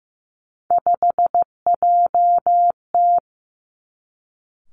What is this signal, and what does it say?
Morse code, heard as a keyed tone, about 15 wpm, 700 Hz. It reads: 5JT
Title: Three Challenging Letter-Number Combinations 15wpm